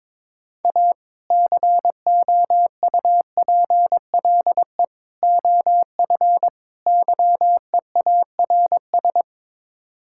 Morse code, heard as a keyed tone, about 22 wpm, 700 Hz. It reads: A COUPLE OF YEARS
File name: practice / 22wpm / Build A Phrase - 5-Word Phrases - 22wpm